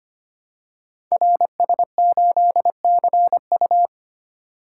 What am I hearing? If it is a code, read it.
Morse code, heard as a keyed tone, about 25 wpm, 700 Hz. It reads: RS8CU
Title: Code Groups - 5 Letter:Numbers - 25wpm